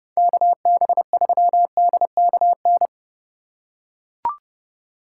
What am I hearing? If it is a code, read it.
Morse code, heard as a keyed tone, about 27 wpm, 700 Hz. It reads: KB3DKN